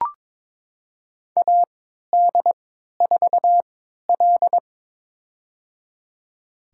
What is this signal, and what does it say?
Morse code, heard as a keyed tone, about 21 wpm, spaced as for 12 wpm, 700 Hz. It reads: AD4L
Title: USA Call Signs - 3x Character Spacing - 22wpm